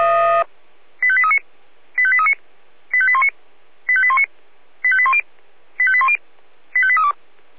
Five-tone selective call (ZVEI), heard as quick runs of five tones, tones of about 70 ms. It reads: 85839 85839 85819 85819 85810 85810 85832